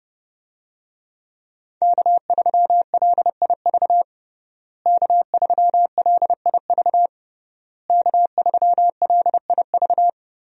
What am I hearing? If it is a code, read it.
Morse code, heard as a keyed tone, about 30 wpm, 700 Hz. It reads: K3LIV K3LIV K3LIV
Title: USA Call Signs - 5 Characters - Repeated 3x - 30wpm